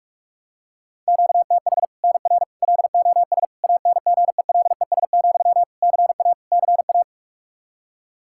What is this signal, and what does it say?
Morse code, heard as a keyed tone, about 47 wpm, 700 Hz. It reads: QTH NR LOS ANGELES, CA CA
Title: Practice QSO - Exchange 1 - Beginning - 45wpm